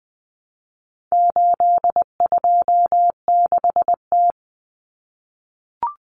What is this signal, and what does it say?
Morse code, heard as a keyed tone, about 20 wpm, 700 Hz. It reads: 826T
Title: Common Cut Numbers - 4-Digit 20wpm